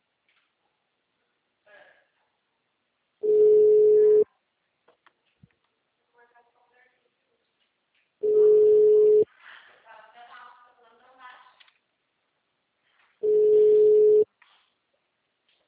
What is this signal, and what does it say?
Ringback tone (Europe, 425 Hz), with speech.
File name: out- -2003- - - .2